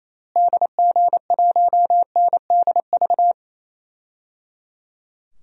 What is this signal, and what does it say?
Morse code, heard as a keyed tone, about 28 wpm, 700 Hz. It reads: DG1NDV
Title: DXCC-100 Call Signs - 6 Characters - Repeated 3x - 28wpm